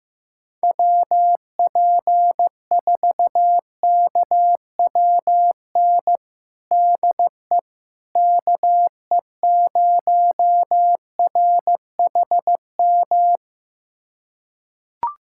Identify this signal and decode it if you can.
Morse code, heard as a keyed tone, about 15 wpm, 700 Hz. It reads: WP4KWN DE KE0RHM
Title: USA Callsign DE Callsign - 6 Characters - 15wpm